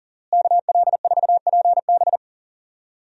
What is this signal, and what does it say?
Morse code, heard as a keyed tone, about 40 wpm, 700 Hz. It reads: KL4PB
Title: USA Call Signs - 5 Characters - Repeated 3x - 40wpm